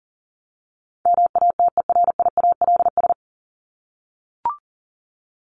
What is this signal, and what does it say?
Morse code, heard as a keyed tone, about 40 wpm, 700 Hz. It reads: MATERIALS
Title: ES:S Suffix - Top 100 Words with Base Words 40wpm